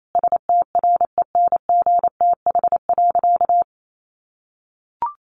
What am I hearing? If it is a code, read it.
Morse code, heard as a keyed tone, about 28 wpm, 700 Hz. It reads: STRENGTH.
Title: Sentences from Top 1300 Words 28wpm